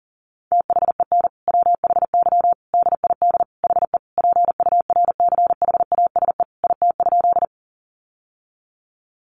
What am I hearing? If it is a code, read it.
Morse code, heard as a keyed tone, about 40 wpm, 700 Hz. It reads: THEN WHY DID HE PURCHASE IT?
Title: Sentences from Top 800 Words 40wpm